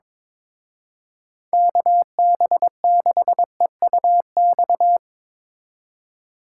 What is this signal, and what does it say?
Morse code, heard as a keyed tone, about 24 wpm, 700 Hz. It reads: KB6EUX
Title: Speed Racing - USA 2x3 Call Signs 22wpm